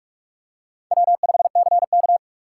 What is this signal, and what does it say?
Morse code, heard as a keyed tone, about 44 wpm, 700 Hz. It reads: W5CK